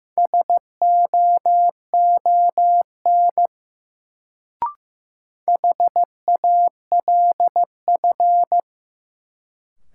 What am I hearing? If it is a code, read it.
Morse code, heard as a keyed tone, about 15 wpm, 700 Hz. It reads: SOON HALF
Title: Sentences from Top 400 Words - New Word Drill - 15wpm